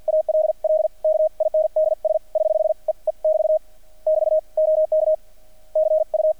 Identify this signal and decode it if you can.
Morse code, heard as a keyed tone, about 50 wpm, spaced as for 44 wpm, 630 Hz. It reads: ALL KINDS <HH> E E <BT> <BT> OK CU
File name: archived / CRSnet